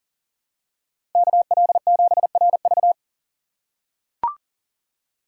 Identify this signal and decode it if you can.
Morse code, heard as a keyed tone, about 40 wpm, 700 Hz. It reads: KL7RV
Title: USA Call Signs - 5 Characters - 2x Character Spacing - 40wpm